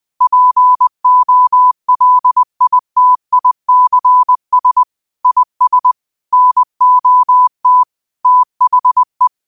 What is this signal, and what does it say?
Morse code, heard as a keyed tone, about 20 wpm, 1000 Hz. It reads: POLITICS IS NOT THE
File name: qotdfiles / 20wpm